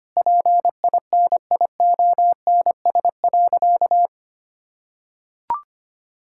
Morse code, heard as a keyed tone, about 25 wpm, 700 Hz. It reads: PINIONS.
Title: Pangram Challenge v2 - 25wpm